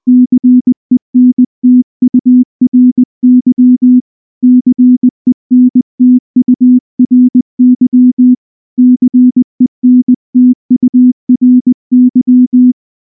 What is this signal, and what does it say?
Morse code, heard as a keyed tone, about 19 wpm, 250 Hz. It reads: CENTURY CENTURY CENTURY
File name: morse / morse-training-20WPM